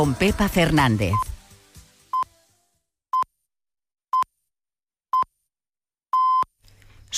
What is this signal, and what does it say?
Radio time signal (time pips), with speech. It is heard from 1.1 to 6.4 s.